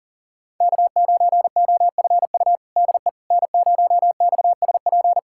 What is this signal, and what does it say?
Morse code, heard as a keyed tone, about 40 wpm, 700 Hz. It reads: K9OFU DE N0XSP
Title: USA Callsign DE Callsign - 5 Characters - 40wpm